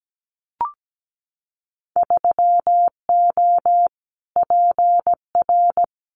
Morse code, heard as a keyed tone, about 17 wpm, 700 Hz. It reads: 3O PR
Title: Winter Field Day Exchanges Encore 17wpm